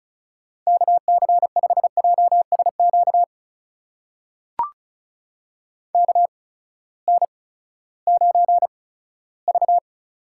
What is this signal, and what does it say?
Morse code, heard as a keyed tone, about 36 wpm, 700 Hz. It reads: K C 5 J S Q K N 9 V
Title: USA Call Signs - 8x Character Spacing - 35wpm